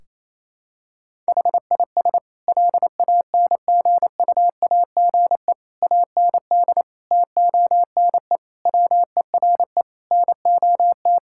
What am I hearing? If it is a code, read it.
Morse code, heard as a keyed tone, about 28 wpm, 700 Hz. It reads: HIS LANGUAGE AND TONE WERE NOT
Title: Sentences from Top 1600 Words 28wpm